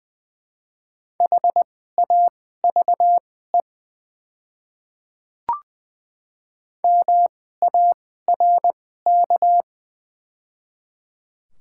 Morse code, heard as a keyed tone, about 21 wpm, spaced as for 15 wpm, 700 Hz. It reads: HAVE MARK
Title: 2x Character Spacing - Top 500 Words - 20wpm